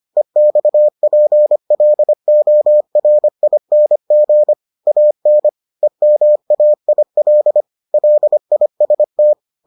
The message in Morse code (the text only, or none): EXPLORING AN EMAIL LIST